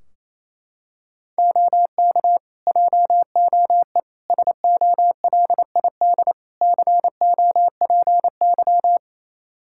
Morse code, heard as a keyed tone, about 28 wpm, 700 Hz. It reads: OK JOE SOLID COPY
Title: K1USN SST QSO Practice 28wpm